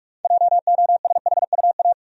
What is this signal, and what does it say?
Morse code, heard as a keyed tone, about 46 wpm, 700 Hz. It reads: JOSHUA
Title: USA Common Names - 45wpm